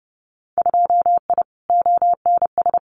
Morse code, heard as a keyed tone, about 30 wpm, 700 Hz. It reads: 2I ONS
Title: Winter Field Day Exchanges Encore 30wpm